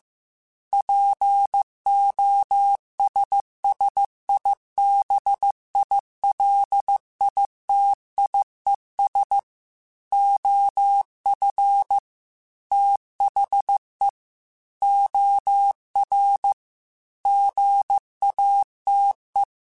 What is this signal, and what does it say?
Morse code, heard as a keyed tone, about 15 wpm, 800 Hz. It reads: POSSIBILITIES OF THE OR GATE
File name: Technical Subjects / Technical Subjects-mp3s / OR Gate @13